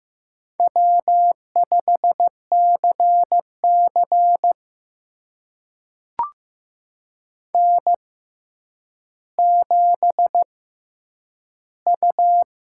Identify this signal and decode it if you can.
Morse code, heard as a keyed tone, about 15 wpm, spaced as for 9 wpm, 700 Hz. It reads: W5CC N7U
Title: USA Call Signs - 4 Characters - 6x Character Spacing - 15wpm